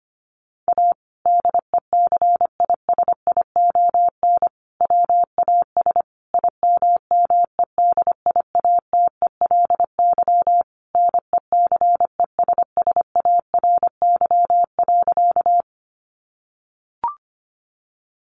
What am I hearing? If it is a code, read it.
Morse code, heard as a keyed tone, about 25 wpm, 700 Hz. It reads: A DECISION WAS IMMEDIATELY NECESSARY.
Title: Sentences from Top 1100 Words 25wpm